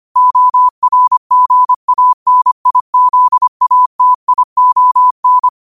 Morse code, heard as a keyed tone, about 25 wpm, 1000 Hz. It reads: ORGANIZATION
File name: qotdfiles / 25wpm